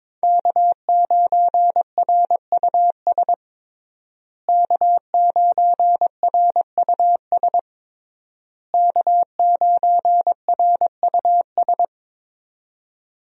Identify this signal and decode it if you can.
Morse code, heard as a keyed tone, about 22 wpm, 700 Hz. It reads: K9RUS K9RUS K9RUS
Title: USA Call Signs - 5 Characters - Repeated 3x - 22wpm